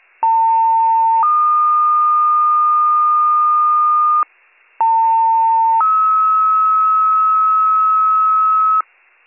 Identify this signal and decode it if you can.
Two-tone page, heard as a one-second tone, then a three-second tone, about 900 then 1220 Hz, about 900 then 1290 Hz. It is heard from 0.2 to 8.8 s.